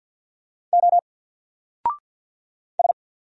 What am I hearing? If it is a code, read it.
Morse code, heard as a keyed tone, about 50 wpm, 700 Hz. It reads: O S
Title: Single Character (Letter:Number:Punctuation:Prosign) 50wpm